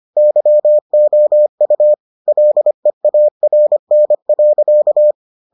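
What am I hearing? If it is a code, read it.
Morse code, heard as a keyed tone, about 25 wpm, 600 Hz. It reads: YOU LEARN.
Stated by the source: Morse edition of the Foundations of Amateur Radio episodes.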